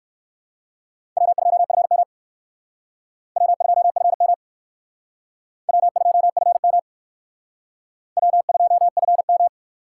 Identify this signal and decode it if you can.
Morse code, heard as a keyed tone, about 60 wpm, 700 Hz. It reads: W2FM W2FM W2FM W2FM